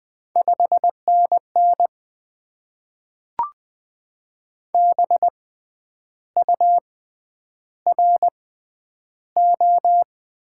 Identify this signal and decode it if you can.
Morse code, heard as a keyed tone, about 20 wpm, spaced as for 9 wpm, 700 Hz. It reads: 5NN BURO
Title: Top 90 QSO Elements - 6x Character Spacing - 20wpm